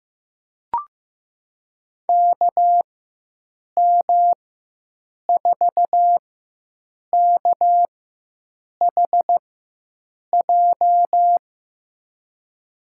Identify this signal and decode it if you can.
Morse code, heard as a keyed tone, about 15 wpm, spaced as for 7 wpm, 700 Hz. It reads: KM4KHJ
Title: USA Call Signs - 4x Character Spacing - 15wpm